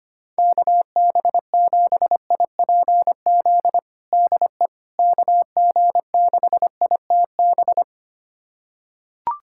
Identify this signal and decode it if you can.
Morse code, heard as a keyed tone, about 25 wpm, 700 Hz. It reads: KB7IPZ DE KG6ITB
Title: USA Callsign DE Callsign 25wpm